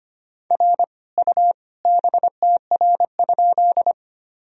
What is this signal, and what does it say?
Morse code, heard as a keyed tone, about 25 wpm, 700 Hz. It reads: R U BTR?
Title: Sentences from Top 400 Words and Abbreviations 25wpm